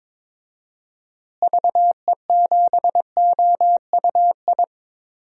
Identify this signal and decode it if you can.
Morse code, heard as a keyed tone, about 22 wpm, 700 Hz. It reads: VE7OUI
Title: DXCC-100 Call Signs - 6 Characters - Repeated 3x - 22wpm